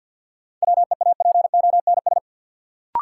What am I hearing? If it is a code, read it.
Morse code, heard as a keyed tone, about 53 wpm, 700 Hz. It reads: WEAPONS
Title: Sets of 5 Words Encore - New Word Drill - 50wpm